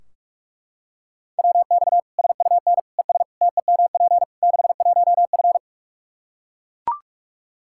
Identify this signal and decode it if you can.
Morse code, heard as a keyed tone, about 41 wpm, 700 Hz. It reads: WX SUN ES TEMP 61F
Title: Practice QSO - Exchange 2 - Info - 45wpm